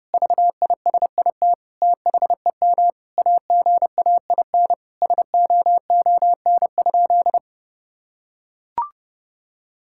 Morse code, heard as a keyed tone, about 30 wpm, 700 Hz. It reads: VISIT THEM AGAIN SOON?
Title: Sentences from Top 1000 Words - Review Part 6 - 30wpm